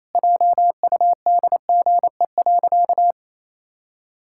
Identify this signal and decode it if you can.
Morse code, heard as a keyed tone, about 28 wpm, 700 Hz. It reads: JUDGE.
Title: Pangram Challenge v2 - 28wpm